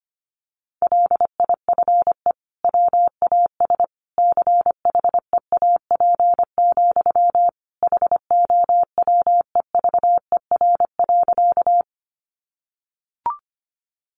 Morse code, heard as a keyed tone, about 25 wpm, 700 Hz. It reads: LIFE WAS CHEAP, HOWEVER.
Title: Sentences from Top 1200 Words 25wpm